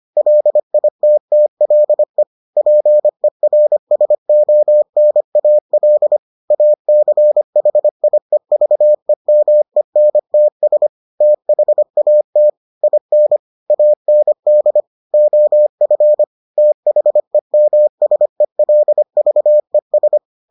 Morse code, heard as a keyed tone, about 25 wpm, 600 Hz. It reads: LITTLE PERSONAL ACHIEVEMENTS THAT IN AND OF THEMSELVES